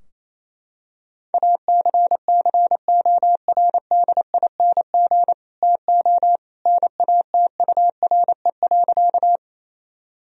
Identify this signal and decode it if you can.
Morse code, heard as a keyed tone, about 27 wpm, 700 Hz. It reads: ACCORDING TO NATURE.
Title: Sentences from Top 1000 Words - Review Part 4 - 28wpm